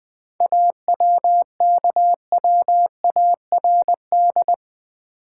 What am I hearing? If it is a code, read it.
Morse code, heard as a keyed tone, about 20 wpm, 700 Hz. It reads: AWKWARD
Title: 7 Letter Words - First 3 Letters Given - 20wpm